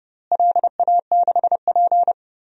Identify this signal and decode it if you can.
Morse code, heard as a keyed tone, about 29 wpm, 700 Hz. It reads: LA6P
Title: DXCC-100 Call Signs - 4 Characters - Repeated 2x - 30wpm